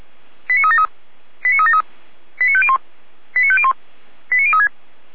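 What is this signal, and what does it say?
Five-tone selective call (ZVEI), heard as quick runs of five tones, tones of about 80 ms. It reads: 89373 89373 89501 89501 89036
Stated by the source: ZVEI five-tone selective call received with an SDR.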